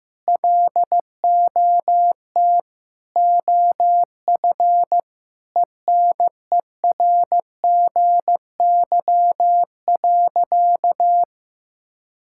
Morse code, heard as a keyed tone, about 15 wpm, 700 Hz. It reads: LOT OF ENERGY.
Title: Sentences from Top 1000 Words - Review Part 6 - 15wpm